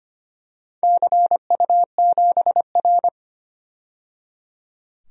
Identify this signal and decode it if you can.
Morse code, heard as a keyed tone, about 25 wpm, 700 Hz. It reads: CU7R
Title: DXCC-100 Call Signs - 4 Characters - 25wpm